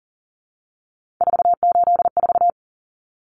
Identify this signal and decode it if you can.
Morse code, heard as a keyed tone, about 40 wpm, 700 Hz. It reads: 484
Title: Numbers - 3-Digit 40wpm